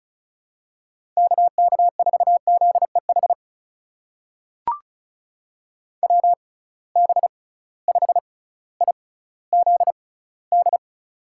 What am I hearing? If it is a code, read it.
Morse code, heard as a keyed tone, about 36 wpm, spaced as for 18 wpm, 700 Hz. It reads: KK4ZEH WB5IZD
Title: USA Call Signs - 6 Characters - 6x Character Spacing - 35wpm